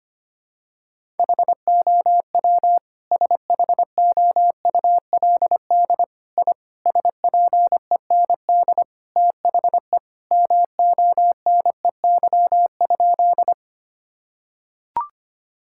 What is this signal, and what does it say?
Morse code, heard as a keyed tone, about 25 wpm, 700 Hz. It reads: HOW SHOULD I SPEND THE MONEY?
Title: Sentences from Top 1000 Words - Review Part 5 - 25wpm